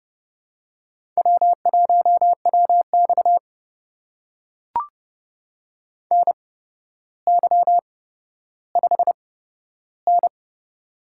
Morse code, heard as a keyed tone, about 30 wpm, spaced as for 13 wpm, 700 Hz. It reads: W1WX NY5N
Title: USA Call Signs - 4 Characters - 8x Character Spacing - 30wpm